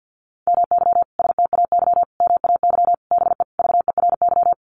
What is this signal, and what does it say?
Morse code, heard as a keyed tone, about 50 wpm, 700 Hz. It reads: MY STAY MAY BE VERY
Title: Sentences from Top 1000 Words 50wpm